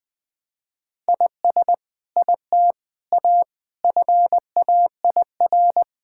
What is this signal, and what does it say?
Morse code, heard as a keyed tone, about 20 wpm, 700 Hz. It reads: IS IT A FAIR
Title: Sentences from Top 1600 Words 20wpm